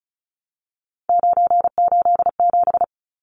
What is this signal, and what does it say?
Morse code, heard as a keyed tone, about 34 wpm, 700 Hz. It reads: 987
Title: Numbers - 3-Digit 35wpm